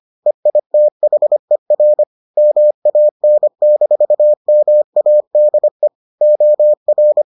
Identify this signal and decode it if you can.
Morse code, heard as a keyed tone, about 25 wpm, 600 Hz. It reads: EITHER MAN-MADE OR